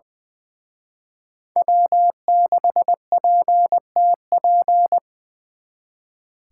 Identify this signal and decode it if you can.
Morse code, heard as a keyed tone, about 24 wpm, 700 Hz. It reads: W6PTP
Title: Speed Racing - USA 1x3 Call Signs 20wpm